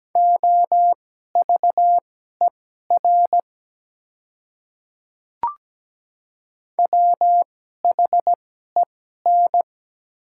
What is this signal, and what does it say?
Morse code, heard as a keyed tone, about 17 wpm, spaced as for 12 wpm, 700 Hz. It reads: OVER WHEN
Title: 2x Character Spacing - Top 100 Words - 17wpm